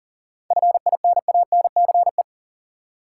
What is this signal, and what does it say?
Morse code, heard as a keyed tone, about 40 wpm, 700 Hz. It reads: FINANCE
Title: Practice QSO - Exchange 3 - More Info - 40wpm